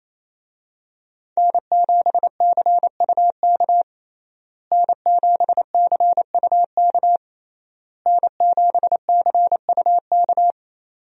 Morse code, heard as a keyed tone, about 28 wpm, 700 Hz. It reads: N7CUK N7CUK N7CUK
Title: USA Call Signs - 5 Characters - Repeated 3x - 28wpm